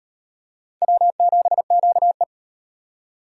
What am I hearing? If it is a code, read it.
Morse code, heard as a keyed tone, about 38 wpm, 700 Hz. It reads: W7QE